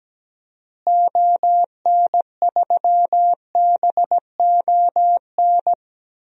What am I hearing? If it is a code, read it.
Morse code, heard as a keyed tone, about 17 wpm, 700 Hz. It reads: ON3BON
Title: DXCC-100 Call Signs - 6 Characters - Repeated 2x - 17wpm